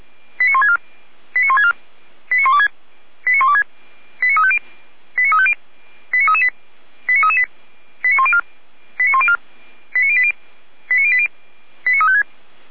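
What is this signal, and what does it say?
Five-tone selective call (ZVEI), heard as quick runs of five tones, tones of about 70 ms. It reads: 89164 89164 89126 89126 89350 89350 89338 89338 89104 89104 89080 89080 89356
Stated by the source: ZVEI five-tone selective call 89701 received on an SDR.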